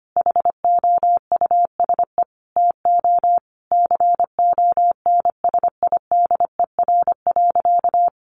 Morse code, heard as a keyed tone, about 25 wpm, 700 Hz. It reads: HOUSE TO CONSIDER.
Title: Sentences from Top 900 Words 25wpm